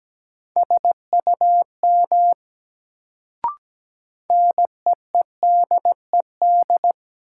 Morse code, heard as a keyed tone, about 17 wpm, 700 Hz. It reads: SUM NEEDED
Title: Sentences from Top 1000 Words - Review Part 6 - New Word Drill - 17wpm